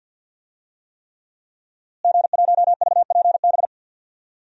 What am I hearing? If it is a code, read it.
Morse code, heard as a keyed tone, about 46 wpm, 700 Hz. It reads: G1VPB
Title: DXCC-100 Call Signs - 5 Characters - Repeated 3x - 50wpm